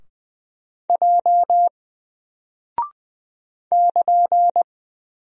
Morse code, heard as a keyed tone, about 20 wpm, 700 Hz. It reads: J <KN>
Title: Single Character (Letter:Number:Punctuation:Prosign) 20wpm